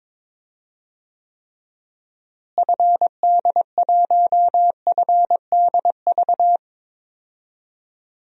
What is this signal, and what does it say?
Morse code, heard as a keyed tone, about 22 wpm, 700 Hz. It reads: FD1FDV
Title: DXCC-100 Call Signs - 6 Characters - Repeated 3x - 22wpm